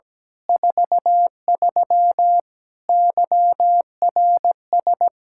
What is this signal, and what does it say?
Morse code, heard as a keyed tone, about 17 wpm, 700 Hz. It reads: 43 YRS
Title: K1USN SST QSO Practice 17wpm